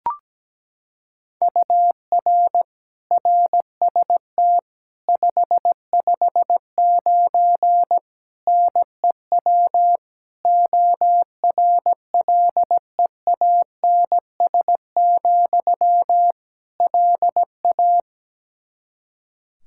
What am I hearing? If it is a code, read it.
Morse code, heard as a keyed tone, about 17 wpm, 700 Hz. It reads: UR RST 559 NEW ORLEANS, LA
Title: Practice QSO - Exchange 1 - Beginning - 17wpm